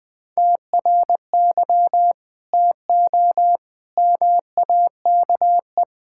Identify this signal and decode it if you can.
Morse code, heard as a keyed tone, about 20 wpm, 700 Hz. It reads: TRY TO MAKE
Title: Sentences with 8-Word Phrases - Spoken Beginning - 20wpm